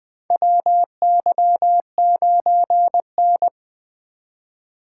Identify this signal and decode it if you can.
Morse code, heard as a keyed tone, about 20 wpm, 700 Hz. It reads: WY9N
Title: USA Call Signs - 3x Character Spacing - 20wpm